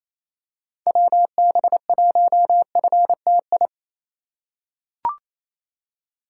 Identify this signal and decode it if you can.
Morse code, heard as a keyed tone, about 27 wpm, 700 Hz. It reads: WB1FTI
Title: USA Call Signs - 6 Characters - 4x Character Spacing - 28wpm